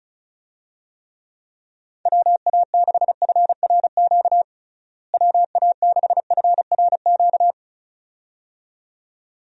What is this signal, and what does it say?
Morse code, heard as a keyed tone, about 36 wpm, 700 Hz. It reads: WA6FRQ WA6FRQ
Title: USA Call Signs - 6 Characters - Repeated 2x - 35wpm